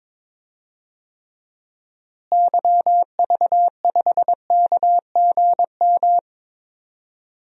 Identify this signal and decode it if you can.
Morse code, heard as a keyed tone, about 21 wpm, 700 Hz. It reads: YV5KGM
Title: DXCC-100 Call Signs - 6 Characters - Repeated 3x - 22wpm